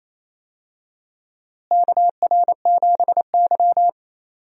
Morse code, heard as a keyed tone, about 27 wpm, 700 Hz. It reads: KR7Y
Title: USA Call Signs - 4 Characters - Repeated 3x - 28wpm